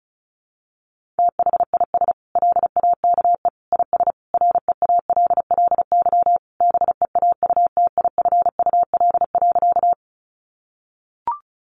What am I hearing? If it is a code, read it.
Morse code, heard as a keyed tone, about 34 wpm, 700 Hz. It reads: THIS LAKE IS REALLY BEAUTIFUL.
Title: Sentences from Top 1100 Words 35wpm